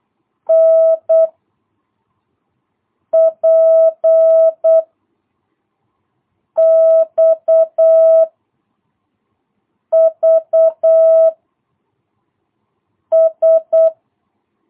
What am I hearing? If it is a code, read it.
Morse code, heard as a keyed tone, about 8 wpm, 660 Hz. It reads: N P X V S
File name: jh8cbh / 2022cw